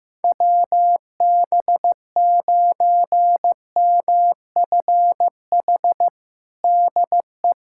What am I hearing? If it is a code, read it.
Morse code, heard as a keyed tone, about 15 wpm, 700 Hz. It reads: WB9MFH DE
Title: USA Callsign DE Callsign - 6 Characters - 15wpm